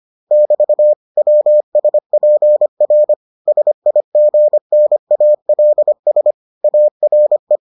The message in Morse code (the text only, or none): <BT> WSPR SIGNALS ARE